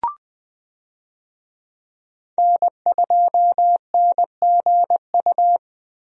Morse code, heard as a keyed tone, about 20 wpm, 700 Hz. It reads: N2NGU
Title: USA Call Signs - 5 Characters - Repeated 3x - 20wpm